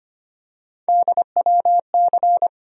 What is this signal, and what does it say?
Morse code, heard as a keyed tone, about 25 wpm, 700 Hz. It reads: DWC
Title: USA Call Sign Postfixes - 3 Letters 25wpm